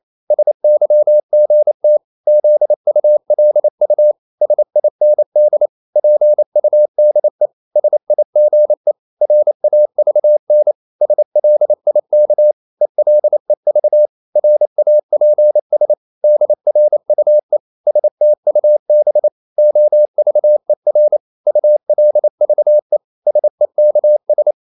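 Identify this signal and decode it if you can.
Morse code, heard as a keyed tone, about 28 wpm, 600 Hz. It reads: SYGT ZULU SIND PUDE SIGE RAVN SLIK ELEV RAPS DRUE STUB OVER ULVE SEKS
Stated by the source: Danish short words sent in Morse at 28 wpm, from CW med Gnister.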